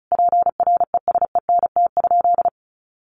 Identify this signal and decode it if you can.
Morse code, heard as a keyed tone, about 34 wpm, 700 Hz. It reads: PRESENT?
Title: Sentences from Top 800 Words 35wpm